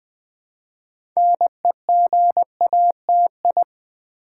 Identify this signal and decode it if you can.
Morse code, heard as a keyed tone, about 20 wpm, 700 Hz. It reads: NEGATI
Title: Build A Word - 2 Letters At A Time - 8-Letter Words - 20wpm